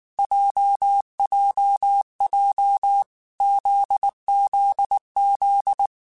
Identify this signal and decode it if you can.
Morse code, heard as a keyed tone, about 19 wpm, 800 Hz. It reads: JJJ ZZZ
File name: Basic Sets / Basic Letter Groups Collective7 / JZ17@18